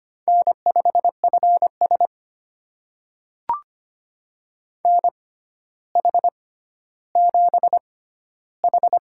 Morse code, heard as a keyed tone, about 25 wpm, spaced as for 13 wpm, 700 Hz. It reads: N5FS NH7H
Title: USA Call Signs - 6x Character Spacing - 25wpm